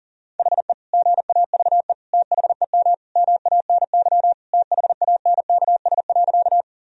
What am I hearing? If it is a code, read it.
Morse code, heard as a keyed tone, about 40 wpm, 700 Hz. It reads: HE GAVE THEM MANY THANKS.
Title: Sentences from Top 1000 Words - Review Part 3 - 40wpm